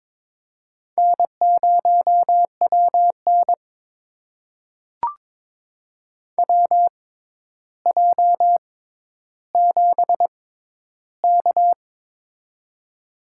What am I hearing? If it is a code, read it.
Morse code, heard as a keyed tone, about 21 wpm, spaced as for 11 wpm, 700 Hz. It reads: N0WN WJ7K
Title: USA Call Signs - 4 Characters - 6x Character Spacing - 22wpm